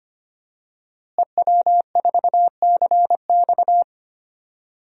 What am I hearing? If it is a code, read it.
Morse code, heard as a keyed tone, about 25 wpm, 700 Hz. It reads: EW4CX
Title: DXCC-100 Call Signs - 5 Characters - Repeated 2x - 25wpm